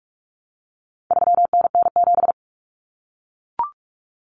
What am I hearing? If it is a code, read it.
Morse code, heard as a keyed tone, about 44 wpm, 700 Hz. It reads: 3NN7
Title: Common Cut Numbers - 4-Digit 45wpm